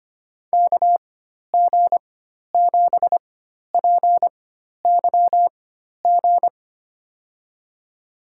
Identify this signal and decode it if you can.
Morse code, heard as a keyed tone, about 25 wpm, spaced as for 12 wpm, 700 Hz. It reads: KG7PYG
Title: USA Call Signs - 6 Characters - 4x Character Spacing - 25wpm